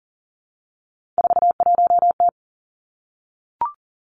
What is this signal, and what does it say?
Morse code, heard as a keyed tone, about 40 wpm, 700 Hz. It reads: 41T
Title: Three Challenging Letter-Number Combinations 40wpm